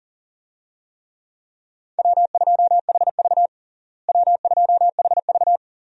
Morse code, heard as a keyed tone, about 40 wpm, 700 Hz. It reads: W2HV W2HV
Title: USA Call Signs - 4 Characters - Repeated 3x - 40wpm